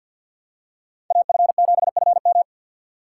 Morse code, heard as a keyed tone, about 48 wpm, 700 Hz. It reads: AF7FM
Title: USA Call Signs - 3x Character Spacing - 50wpm